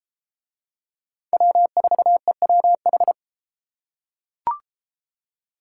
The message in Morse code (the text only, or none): W4EWH